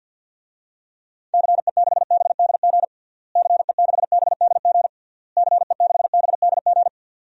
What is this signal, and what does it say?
Morse code, heard as a keyed tone, about 52 wpm, 700 Hz. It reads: CE6BDG CE6BDG CE6BDG
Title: DXCC-100 Call Signs - 6 Characters - Repeated 3x - 50wpm